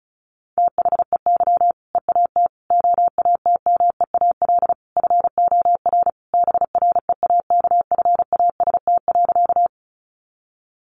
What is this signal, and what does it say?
Morse code, heard as a keyed tone, about 34 wpm, 700 Hz. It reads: THEY EAT OATMEAL FOR BREAKFAST.
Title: Easy Phrases 35wpm